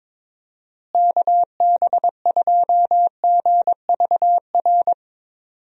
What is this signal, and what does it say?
Morse code, heard as a keyed tone, about 22 wpm, 700 Hz. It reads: KB2GVR
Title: USA Call Signs - 6 Characters - Repeated 3x - 22wpm